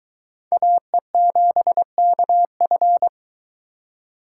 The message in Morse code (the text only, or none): AE7KF